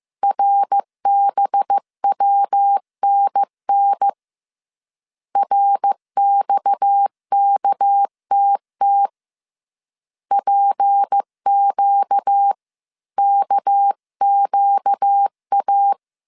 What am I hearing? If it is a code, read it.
Morse code, heard as a keyed tone, about 15 wpm, 800 Hz. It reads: RBWNN RXKTT PQ KQA
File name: CW / DET / 1 6 1500C S C 14wpm 2951